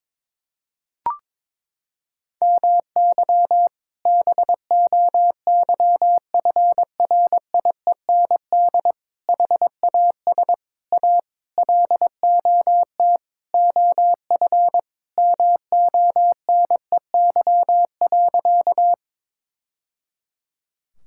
Morse code, heard as a keyed tone, about 22 wpm, 700 Hz. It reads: MY BOYFRIEND HAS A LOT OF MONEY.
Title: Sentences from Top 1000 Words - Review Part 5 - 22wpm